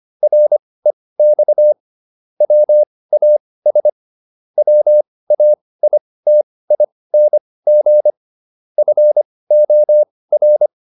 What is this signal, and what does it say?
Morse code, heard as a keyed tone, about 25 wpm, spaced as for 18 wpm, 600 Hz. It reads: REX WAS WAITING FOR